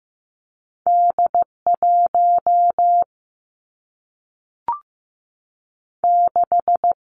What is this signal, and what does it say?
Morse code, heard as a keyed tone, about 15 wpm, 700 Hz. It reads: D 1 6
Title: Two Challenging Letter-Number Combinations 15wpm